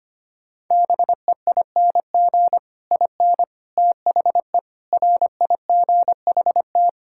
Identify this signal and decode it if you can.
Morse code, heard as a keyed tone, about 25 wpm, 700 Hz. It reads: BEING IN THE RIGHT
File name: Sets of 9 Words - 25wpm